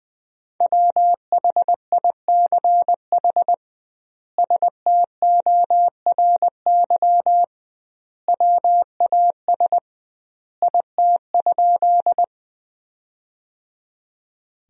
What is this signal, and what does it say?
Morse code, heard as a keyed tone, about 21 wpm, 700 Hz. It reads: WHICH STORY WAS IT?
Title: 2x Word-Spacing - Sentences from Top 300 Words - 20wpm